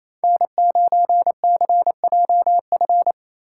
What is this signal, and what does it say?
Morse code, heard as a keyed tone, about 27 wpm, 700 Hz. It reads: N9CJF
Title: USA Call Signs - 5 Characters - 8x Character Spacing - 28wpm